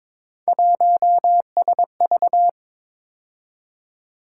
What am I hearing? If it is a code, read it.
Morse code, heard as a keyed tone, about 22 wpm, 700 Hz. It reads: 1SV
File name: practice / 22wpm / Three Letter-Number Combinations - 22wpm